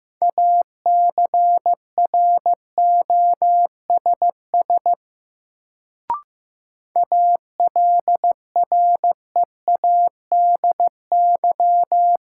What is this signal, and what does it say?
Morse code, heard as a keyed tone, about 15 wpm, 700 Hz. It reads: ACROSS ALREADY
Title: Sentences from Top 700 Words - New Word Drill - 15wpm